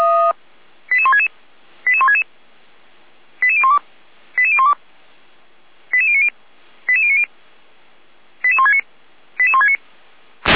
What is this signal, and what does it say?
Five-tone selective call (ZVEI), heard as quick runs of five tones, tones of about 70 ms. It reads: 88166 88166 88012 88012 88090 88090 88169 88169